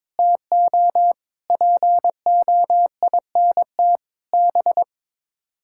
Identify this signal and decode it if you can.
Morse code, heard as a keyed tone, about 22 wpm, 700 Hz. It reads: TO POINT B
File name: practice / 22wpm / Sets of 7 Words - 22wpm